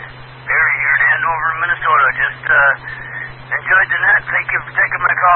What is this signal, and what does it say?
Emergency Alert System (SAME) tones (header bursts), heard as harsh buzzing bursts, with speech. It is heard from 0.5 to 4.7 s.